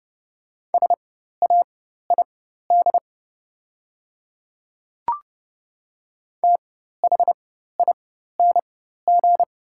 Morse code, heard as a keyed tone, about 30 wpm, spaced as for 14 wpm, 700 Hz. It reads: SAID THING
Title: 4x Character Spacing - Top 100 Words - 30wpm